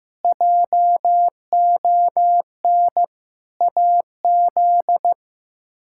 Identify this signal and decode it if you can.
Morse code, heard as a keyed tone, about 15 wpm, 700 Hz. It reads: JON AZ
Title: K1USN SST QSO Practice - 15wpm